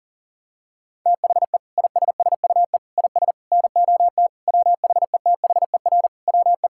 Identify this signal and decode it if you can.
Morse code, heard as a keyed tone, about 40 wpm, 700 Hz. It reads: THE ISSUE IS NOT WHETHER WE
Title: Sentences from Top 1000 Words - Review Part 6 - 40wpm